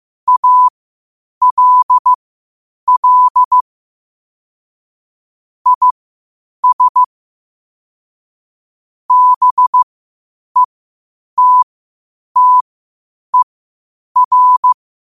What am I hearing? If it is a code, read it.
Morse code, heard as a keyed tone, about 15 wpm, spaced as for 8 wpm, 1000 Hz. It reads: ALL IS BETTER
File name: qotdfiles / 8wpm